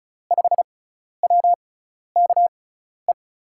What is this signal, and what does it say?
Morse code, heard as a keyed tone, about 36 wpm, spaced as for 12 wpm, 700 Hz. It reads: 5WKE
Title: USA Call Signs - 6 Characters - 6x Character Spacing - 35wpm